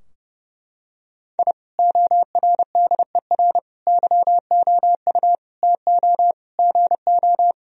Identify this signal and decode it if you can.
Morse code, heard as a keyed tone, about 32 wpm, 700 Hz. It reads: I ORDER YOU TO GO
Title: Sentences from Top 1000 Words - Review Part 2 - 30wpm